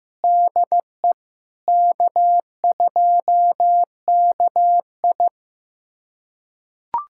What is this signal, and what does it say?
Morse code, heard as a keyed tone, about 15 wpm, 700 Hz. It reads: DE K2KI
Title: USA Callsign DE Callsign - 4 Characters - 15wpm